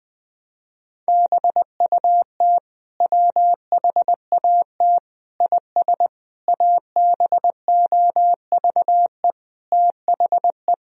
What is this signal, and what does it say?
Morse code, heard as a keyed tone, about 19 wpm, 700 Hz. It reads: BUT WHAT IS ABOVE THE
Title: Sentences from Top 1000 Words - Review Part 2 - 20wpm